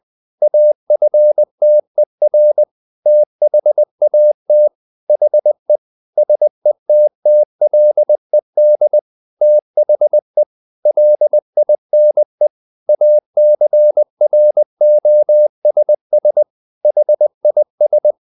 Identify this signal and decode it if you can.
Morse code, heard as a keyed tone, about 20 wpm, 600 Hz. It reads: AFTER THAT HE SETTLED THE LINE ACROSS HIS
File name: the old man and the sea part 2 20wpm